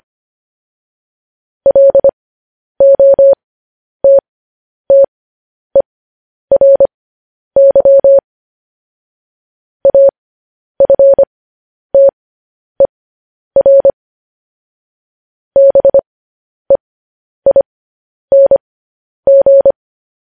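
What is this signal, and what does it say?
Morse code, heard as a keyed tone, about 25 wpm, spaced as for 10 wpm, 550 Hz. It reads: LOTTERY AFTER BEING
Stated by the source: Morse code news headlines sent at 10.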